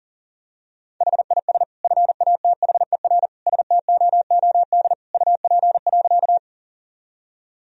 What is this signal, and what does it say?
Morse code, heard as a keyed tone, about 40 wpm, 700 Hz. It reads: HIS FATHER STOOD UP.
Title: Sentences from Top 1000 Words - Review Part 2 - 40wpm